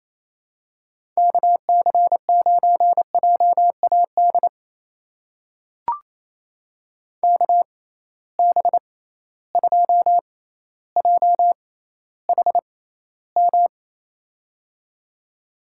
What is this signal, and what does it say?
Morse code, heard as a keyed tone, about 28 wpm, spaced as for 14 wpm, 700 Hz. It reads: KC9JAD KB2JHM
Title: USA Call Signs - 6 Characters - 6x Character Spacing - 28wpm